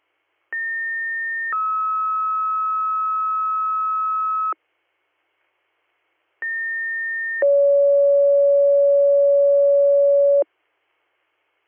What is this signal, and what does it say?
Two-tone page, heard as a one-second tone, then a three-second tone, about 1800 then 1290 Hz, about 1800 then 560 Hz. It is heard from 0.5 to 10.4 s.